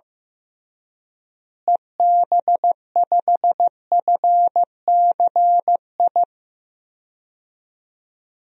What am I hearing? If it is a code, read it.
Morse code, heard as a keyed tone, about 15 wpm, 700 Hz. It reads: EB5FCI
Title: DXCC-100 Call Signs - 6 Characters - Repeated 3x - 15wpm